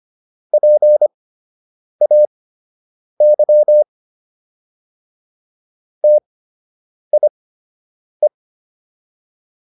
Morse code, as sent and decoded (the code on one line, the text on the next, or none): .--. .- -.-- / - .. .
PAY TIE